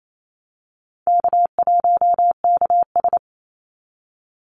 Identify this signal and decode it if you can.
Morse code, heard as a keyed tone, about 28 wpm, 700 Hz. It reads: K1KS
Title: Easy Call Signs Repeated 2x - 28wpm